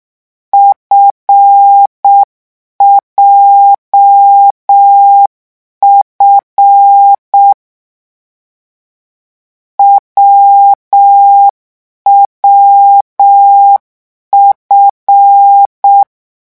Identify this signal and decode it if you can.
Morse code, heard as a keyed tone, about 6 wpm, 800 Hz. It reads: FJF WWF